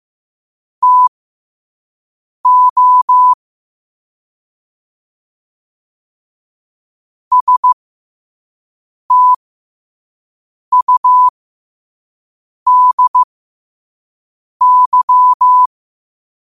Morse code, heard as a keyed tone, about 15 wpm, spaced as for 5 wpm, 1000 Hz. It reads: TO STUDY